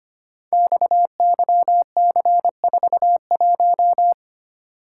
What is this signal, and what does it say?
Morse code, heard as a keyed tone, about 25 wpm, 700 Hz. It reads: XYC41
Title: Code Groups - 5 Letter:Numbers - 25wpm